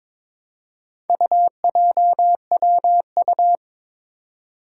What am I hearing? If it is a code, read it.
Morse code, heard as a keyed tone, about 22 wpm, 700 Hz. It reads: UJWU
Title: Four Challenging Letter-Number Combinations 22wpm